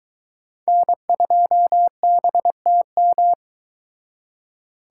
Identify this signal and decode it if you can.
Morse code, heard as a keyed tone, about 23 wpm, 700 Hz. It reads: N2BTM